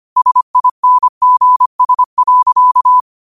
Morse code, heard as a keyed tone, about 25 wpm, 1000 Hz. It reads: SINGS.
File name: qotdfiles / 25wpm